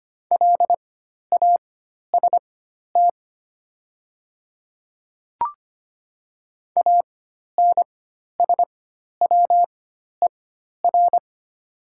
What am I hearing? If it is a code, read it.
Morse code, heard as a keyed tone, about 25 wpm, spaced as for 12 wpm, 700 Hz. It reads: LAST ANSWER
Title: 4x Character Spacing - Top 300 Words - 25wpm